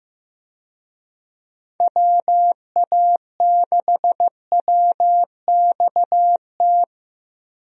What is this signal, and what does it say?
Morse code, heard as a keyed tone, about 15 wpm, 700 Hz. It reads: WA6WXT
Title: USA Call Signs - 6 Characters - Repeated 2x - 15wpm